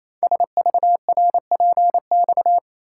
Morse code, heard as a keyed tone, about 29 wpm, 700 Hz. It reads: SVRPX
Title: Code Groups - 5 Letters - 28wpm